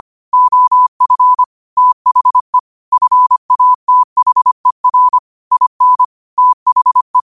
Morse code, heard as a keyed tone, about 25 wpm, 1000 Hz. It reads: OF THE FATHER IN THE
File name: qotdfiles / 25wpm